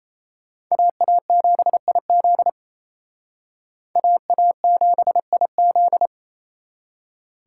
Morse code, heard as a keyed tone, about 35 wpm, 700 Hz. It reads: AA7IZ AA7IZ
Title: Speed Racing - USA 2x2 Call Signs 28wpm